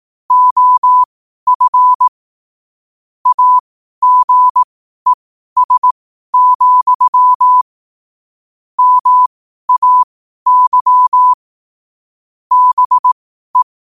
Morse code, heard as a keyed tone, about 18 wpm, spaced as for 12 wpm, 1000 Hz. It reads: OF AGES, MAY BE
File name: qotdfiles / 12wpm